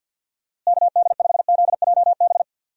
Morse code, heard as a keyed tone, about 48 wpm, 700 Hz. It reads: KD5ZJB
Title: USA Call Signs - 6 Characters - Repeated 3x - 50wpm